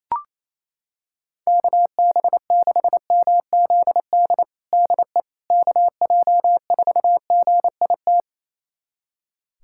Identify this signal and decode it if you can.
Morse code, heard as a keyed tone, about 28 wpm, 700 Hz. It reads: KB6MZD DE KJ4GIT
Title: USA Callsign DE Callsign - 6 Characters - 28wpm